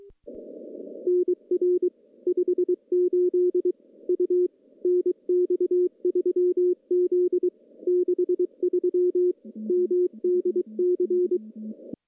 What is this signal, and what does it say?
Morse code, heard as a keyed tone, about 23 wpm, 340 Hz. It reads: NR 58 U NX3Z 63 MDC
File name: CW-SS-k5tr